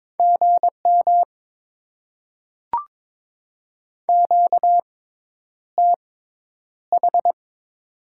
Morse code, heard as a keyed tone, about 21 wpm, spaced as for 10 wpm, 700 Hz. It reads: GM QTH
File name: Top 20 QSO Elements - 6x Character Spacing - 22wpm